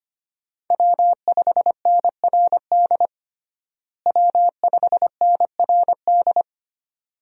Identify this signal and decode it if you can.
Morse code, heard as a keyed tone, about 25 wpm, 700 Hz. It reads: W5NRD W5NRD
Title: USA Call Signs - 5 Characters - Repeated 2x - 25wpm